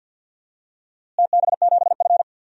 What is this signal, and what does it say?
Morse code, heard as a keyed tone, about 52 wpm, 700 Hz. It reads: TB7F
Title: DXCC-100 Call Signs - 4 Characters - Repeated 3x - 50wpm